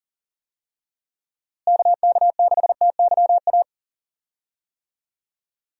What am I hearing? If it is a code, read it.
Morse code, heard as a keyed tone, about 40 wpm, 700 Hz. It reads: KK6TYA
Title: USA Call Signs - 6 Characters - Repeated 2x - 40wpm